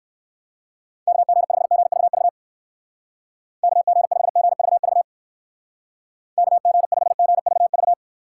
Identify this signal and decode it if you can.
Morse code, heard as a keyed tone, about 58 wpm, 700 Hz. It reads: KG5GVV KG5GVV KG5GVV